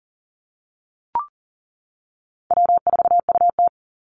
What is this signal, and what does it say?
Morse code, heard as a keyed tone, about 40 wpm, 700 Hz. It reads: W4UT
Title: Easy Call Signs - 40wpm